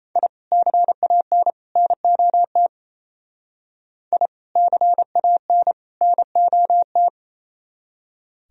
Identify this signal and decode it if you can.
Morse code, heard as a keyed tone, about 33 wpm, 700 Hz. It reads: I CAN NOT I CAN NOT
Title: Speed-Racing - Sets of 3 Words 28wpm